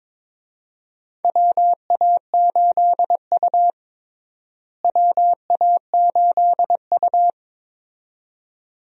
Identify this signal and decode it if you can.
Morse code, heard as a keyed tone, about 21 wpm, 700 Hz. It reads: WA8U WA8U
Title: USA Call Signs - 4 Characters - Repeated 2x - 22wpm